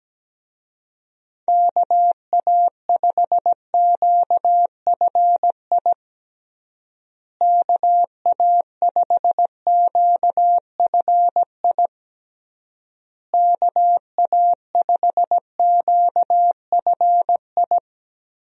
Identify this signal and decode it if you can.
Morse code, heard as a keyed tone, about 17 wpm, 700 Hz. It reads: KA5QFI KA5QFI KA5QFI
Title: USA Call Signs - 6 Characters - Repeated 3x - 17wpm